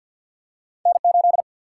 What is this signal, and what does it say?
Morse code, heard as a keyed tone, about 54 wpm, 700 Hz. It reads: N8
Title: Single-Letter Single-Number 50wpm